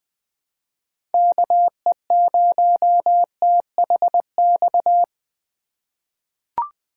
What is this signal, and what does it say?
Morse code, heard as a keyed tone, about 20 wpm, 700 Hz. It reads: KE0THX
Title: USA Call Signs - 6 Characters - 3x Character Spacing - 20wpm